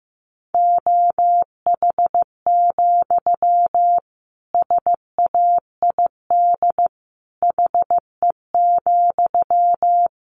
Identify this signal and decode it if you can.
Morse code, heard as a keyed tone, about 15 wpm, 700 Hz. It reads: OH, SAID HE,
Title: Sentences from Top 500 Words 15wpm